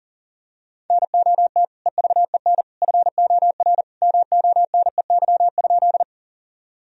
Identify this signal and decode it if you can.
Morse code, heard as a keyed tone, about 40 wpm, 700 Hz. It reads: NOT EVEN FOR MONEY?
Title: Sentences from Top 1000 Words - Review Part 1 - 40wpm